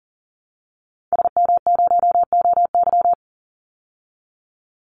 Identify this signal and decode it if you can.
Morse code, heard as a keyed tone, about 40 wpm, 700 Hz. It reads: SM0OY
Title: Call Signs Repeated 3x - 40wpm